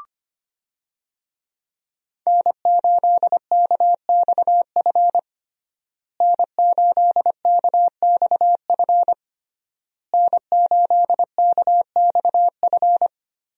Morse code, heard as a keyed tone, about 25 wpm, 700 Hz. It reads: N8KXF N8KXF N8KXF
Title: USA Call Signs - 5 Characters - Repeated 3x - 25wpm